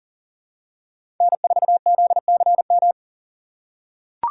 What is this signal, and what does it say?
Morse code, heard as a keyed tone, about 40 wpm, 700 Hz. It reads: N4ZCM
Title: USA Call Signs - 5 Characters - 2x Character Spacing - 40wpm